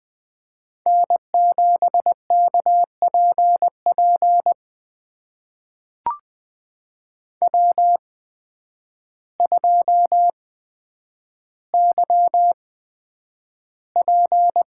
Morse code, heard as a keyed tone, about 20 wpm, spaced as for 9 wpm, 700 Hz. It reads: N7KPP W2YP
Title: USA Call Signs - 5 Characters - 8x Character Spacing - 20wpm